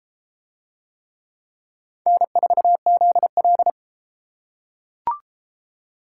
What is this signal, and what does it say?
Morse code, heard as a keyed tone, about 30 wpm, 700 Hz. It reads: N4ZL